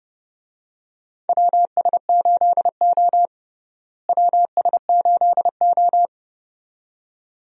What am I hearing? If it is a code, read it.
Morse code, heard as a keyed tone, about 30 wpm, 700 Hz. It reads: WS8O WS8O
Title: USA Call Signs - 4 Characters - Repeated 2x - 30wpm